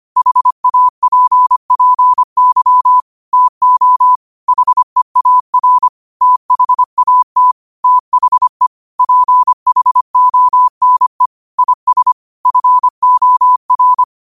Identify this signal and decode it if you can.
Morse code, heard as a keyed tone, about 25 wpm, 1000 Hz. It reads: HAPPY TO HEAR THAT THE PHONE IS FOR
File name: qotdfiles / 25wpm